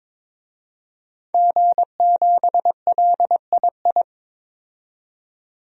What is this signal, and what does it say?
Morse code, heard as a keyed tone, about 22 wpm, 700 Hz. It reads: G7LII
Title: DXCC-100 Call Signs - 5 Characters - 22wpm